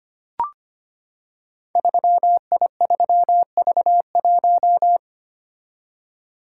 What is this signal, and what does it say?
Morse code, heard as a keyed tone, about 25 wpm, 700 Hz. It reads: 3I3V1
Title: Code Groups - 5 Letter:Numbers - 25wpm